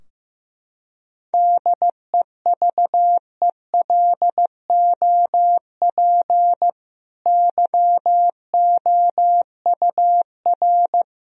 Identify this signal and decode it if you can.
Morse code, heard as a keyed tone, about 16 wpm, 700 Hz. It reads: DEVELOP YOUR
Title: Sentences from Top 1000 Words - Review Part 2 - 15wpm